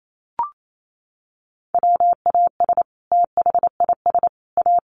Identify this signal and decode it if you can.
Morse code, heard as a keyed tone, about 28 wpm, 700 Hz. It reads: WAS THIS A
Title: Sentences from Top 1200 Words 28wpm